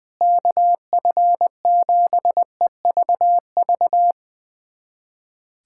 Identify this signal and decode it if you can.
Morse code, heard as a keyed tone, about 21 wpm, 700 Hz. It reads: KF7EVV
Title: USA Callsign DE Callsign - 6 Characters - 20wpm